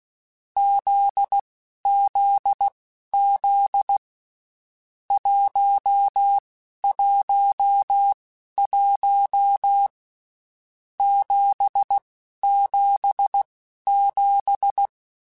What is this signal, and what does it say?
Morse code, heard as a keyed tone, about 16 wpm, spaced as for 11 wpm, 800 Hz. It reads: ZZZ 111 777
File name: Basic Sets / Basic Letter Groups Collective7 / JZ17@10